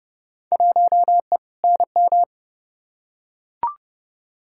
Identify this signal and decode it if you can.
Morse code, heard as a keyed tone, about 30 wpm, 700 Hz. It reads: 1E NM
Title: Field Day Exchanges - 30wpm